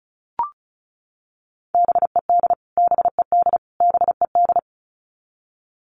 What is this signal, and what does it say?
Morse code, heard as a keyed tone, about 34 wpm, 700 Hz. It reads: BED BED BED
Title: Top 500 Words Repeated 35wpm